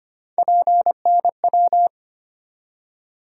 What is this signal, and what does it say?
Morse code, heard as a keyed tone, about 25 wpm, 700 Hz. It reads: PNW
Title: USA Call Sign Postfixes - 3 Letters 25wpm